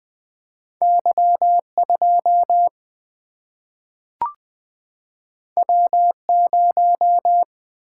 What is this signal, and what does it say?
Morse code, heard as a keyed tone, about 20 wpm, 700 Hz. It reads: Y 2 W 0
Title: Single-Letter Single-Number 20wpm